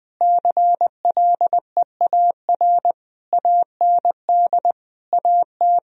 Morse code, heard as a keyed tone, about 20 wpm, 700 Hz. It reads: CLEAR AND AT
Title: Sentences from Top 1000 Words - Review Part 1 - 20wpm